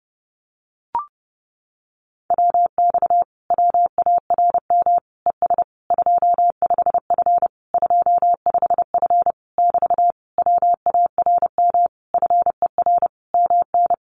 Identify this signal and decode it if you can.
Morse code, heard as a keyed tone, about 30 wpm, 700 Hz. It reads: WX WARM ES 25F 25F <BT> WARM FER MN
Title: Common QSO Phrases Encore 30wpm